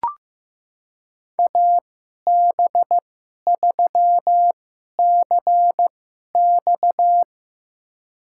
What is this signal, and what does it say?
Morse code, heard as a keyed tone, about 15 wpm, spaced as for 11 wpm, 700 Hz. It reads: AB3CX
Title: USA Call Signs - 5 Characters - 2x Character Spacing - 15wpm